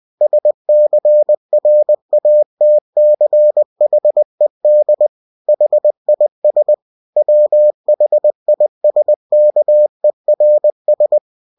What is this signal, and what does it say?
Morse code, heard as a keyed tone, about 20 wpm, 600 Hz. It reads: SCRATCHED HIS WHISKERS
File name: cw-books / winnie-the-pooh / winnie-20wpm-page-0018